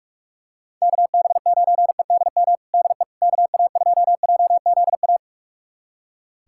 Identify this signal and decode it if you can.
Morse code, heard as a keyed tone, about 46 wpm, 700 Hz. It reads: KB9EDM DE KA2JZA
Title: USA Callsign DE Callsign 45wpm